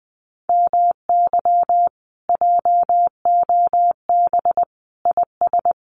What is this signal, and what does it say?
Morse code, heard as a keyed tone, about 20 wpm, 700 Hz. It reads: MY JOB IS
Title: Common QSO Phrases Encore 20wpm